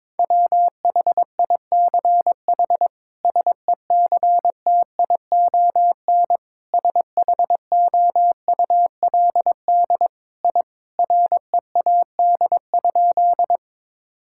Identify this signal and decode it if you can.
Morse code, heard as a keyed tone, about 22 wpm, 700 Hz. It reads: WHICH SECTION SHOULD I READ?
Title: Sentences from Top 1000 Words - Review Part 5 - 22wpm